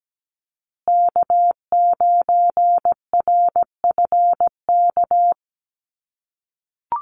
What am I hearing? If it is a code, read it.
Morse code, heard as a keyed tone, about 17 wpm, 700 Hz. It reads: K9RFK
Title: USA Call Signs Repeated 3x 17wpm